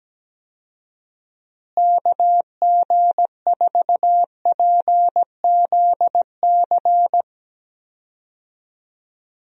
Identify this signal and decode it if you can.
Morse code, heard as a keyed tone, about 17 wpm, 700 Hz. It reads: KG4PZC
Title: USA Call Signs - 6 Characters - Repeated 3x - 17wpm